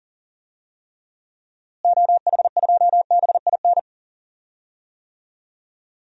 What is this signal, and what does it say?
Morse code, heard as a keyed tone, about 40 wpm, 700 Hz. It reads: OH2BIN
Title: DXCC-100 Call Signs - 6 Characters - Repeated 3x - 40wpm